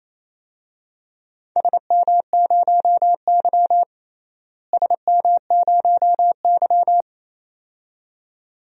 Morse code, heard as a keyed tone, about 28 wpm, 700 Hz. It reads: SM0Y SM0Y
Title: DXCC-100 Call Signs - 4 Characters - Repeated 2x - 28wpm